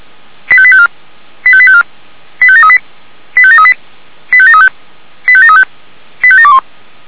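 Five-tone selective call (ZVEI), heard as quick runs of five tones, tones of about 70 ms. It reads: 85854 85854 85738 85738 85735 85735 85721
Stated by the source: ZVEI five-tone selective call received on SDR.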